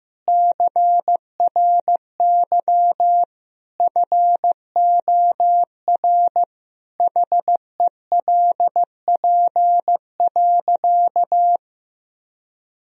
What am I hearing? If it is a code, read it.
Morse code, heard as a keyed tone, about 15 wpm, 700 Hz. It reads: CRY FOR HELP.
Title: Sentences from Top 1000 Words - Review Part 3 - 15wpm